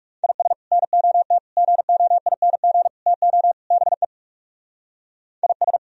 Morse code, heard as a keyed tone, about 36 wpm, 700 Hz. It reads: IS NOT GOING TO BE IS
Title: Speed-Racing - Sets of 5 Words 30wpm